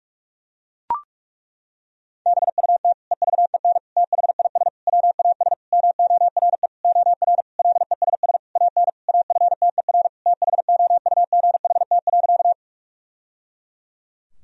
Morse code, heard as a keyed tone, about 41 wpm, 700 Hz. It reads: BUT EVEN THIS WAS MORE OR LESS AN AFTER THOUGHT.
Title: Sentences from Top 1000 Words - Review Part 2 - 45wpm